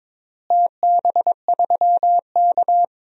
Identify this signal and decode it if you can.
Morse code, heard as a keyed tone, about 21 wpm, 700 Hz. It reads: TB3K
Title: DXCC-100 Call Signs - 4 Characters - Repeated 2x - 22wpm